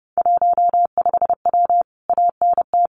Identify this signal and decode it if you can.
Morse code, heard as a keyed tone, about 28 wpm, 700 Hz. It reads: 15W ANT
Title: Common QSO Phrases Encore 30wpm